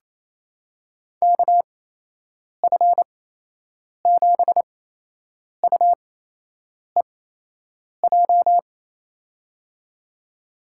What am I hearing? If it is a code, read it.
Morse code, heard as a keyed tone, about 28 wpm, spaced as for 8 wpm, 700 Hz. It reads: KF7UEJ
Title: USA Call Signs - 6 Characters - 8x Character Spacing - 28wpm